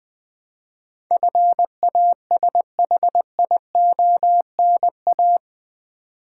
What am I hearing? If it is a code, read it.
Morse code, heard as a keyed tone, about 20 wpm, 700 Hz. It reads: FASHIONA
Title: Build A Word - 4 Letters At A Time - 11-Letter Words - 20wpm